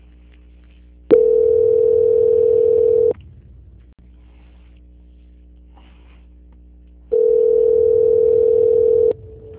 Ringback tone (North America), 440 + 480 Hz.
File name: out- -2027- - - .69